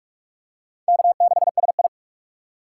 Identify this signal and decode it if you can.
Morse code, heard as a keyed tone, about 46 wpm, 700 Hz. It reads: K6SI
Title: USA Call Signs - 4 Characters - 2x Character Spacing - 45wpm